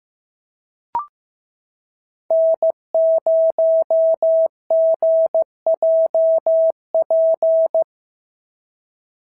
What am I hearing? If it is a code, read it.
Morse code, heard as a keyed tone, about 15 wpm, 650 Hz. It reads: N0GJP
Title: SOTA QSO Practice 15wpm